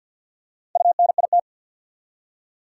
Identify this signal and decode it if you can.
Morse code, heard as a keyed tone, about 48 wpm, 700 Hz. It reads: UNIT
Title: Four Letter Words - 50wpm